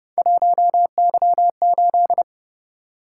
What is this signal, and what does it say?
Morse code, heard as a keyed tone, about 29 wpm, 700 Hz. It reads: 1Y8